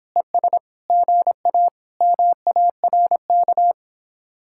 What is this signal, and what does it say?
Morse code, heard as a keyed tone, about 26 wpm, 700 Hz. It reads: ES GA MARK